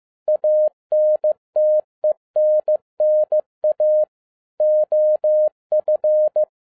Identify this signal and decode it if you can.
Morse code, heard as a keyed tone, about 15 wpm, 610 Hz. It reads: ANTENNA OF
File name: 2012-01-mp3 / n07-ARA JR0DIL-15wpm